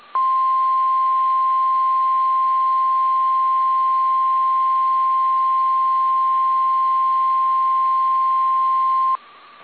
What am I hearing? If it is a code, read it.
Emergency Alert System (SAME) tones (the 1050 Hz attention tone). It is heard from 0.3 to 9.2 s.
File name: eas / KSDZKDJL / KSDZKDJL